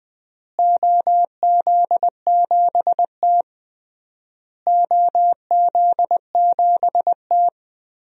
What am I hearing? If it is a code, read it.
Morse code, heard as a keyed tone, about 20 wpm, 700 Hz. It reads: OZ7T OZ7T
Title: DXCC-100 Call Signs - 4 Characters - Repeated 3x - 20wpm